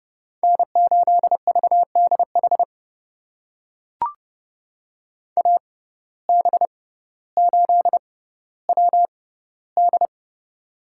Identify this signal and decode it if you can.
Morse code, heard as a keyed tone, about 30 wpm, spaced as for 15 wpm, 700 Hz. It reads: N8VDH AB8WD
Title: USA Call Signs - 5 Characters - 6x Character Spacing - 30wpm